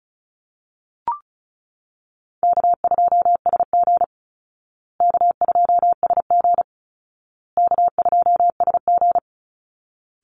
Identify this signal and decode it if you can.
Morse code, heard as a keyed tone, about 34 wpm, 700 Hz. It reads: K2SG K2SG K2SG
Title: Call Signs Repeated 2x - 35wpm